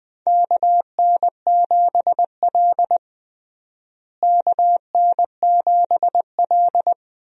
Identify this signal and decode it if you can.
Morse code, heard as a keyed tone, about 20 wpm, 700 Hz. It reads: KN7L KN7L
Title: USA Call Signs - 4 Characters - Repeated 3x - 20wpm